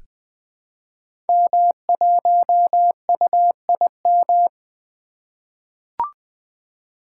Morse code, heard as a keyed tone, about 20 wpm, 700 Hz. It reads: M1UIM
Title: Code Groups - 5 Letter:Numbers - 20wpm